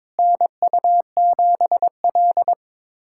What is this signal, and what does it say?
Morse code, heard as a keyed tone, about 22 wpm, 700 Hz. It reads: NU7L
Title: USA Callsign DE Callsign - 4 Characters - 22wpm